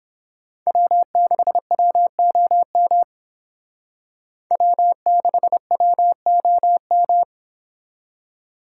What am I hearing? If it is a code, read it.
Morse code, heard as a keyed tone, about 30 wpm, 700 Hz. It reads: W6WOM W6WOM